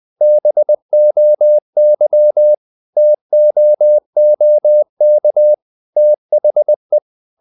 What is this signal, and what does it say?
Morse code, heard as a keyed tone, about 20 wpm, 600 Hz. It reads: BOY TOOK THE
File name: the old man and the sea / the old man and the sea part 1 20wpm